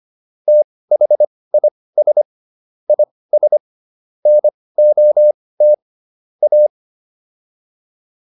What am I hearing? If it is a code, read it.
Morse code, heard as a keyed tone, about 25 wpm, spaced as for 18 wpm, 600 Hz. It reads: THIS IS NOT A